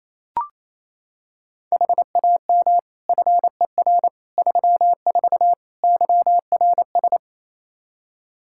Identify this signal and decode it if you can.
Morse code, heard as a keyed tone, about 27 wpm, 700 Hz. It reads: HAM FER 34 YRS
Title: Practice QSO - Exchange 3 - More Info - 28wpm